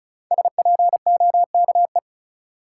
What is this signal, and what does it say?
Morse code, heard as a keyed tone, about 34 wpm, 700 Hz. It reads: SPOKE
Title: Sentences from Top 1200 Words - New Word Drill - 35wpm